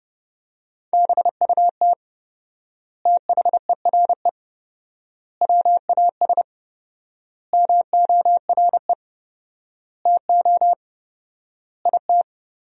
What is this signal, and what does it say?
Morse code, heard as a keyed tone, about 31 wpm, 700 Hz. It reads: BUT THERE WAS MORE TO IT
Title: 4x Word-Spacing - Sentences from Top 100 Words - 30wpm